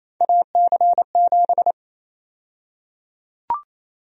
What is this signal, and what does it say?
Morse code, heard as a keyed tone, about 28 wpm, 700 Hz. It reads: AC7
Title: USA Call Sign Prefixes - Repeated 2x - 28wpm